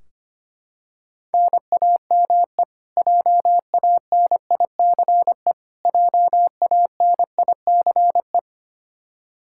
Morse code, heard as a keyed tone, about 25 wpm, 700 Hz. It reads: NAME JANICE JANICE
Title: Practice QSO - Exchange 1 - Beginning - 25wpm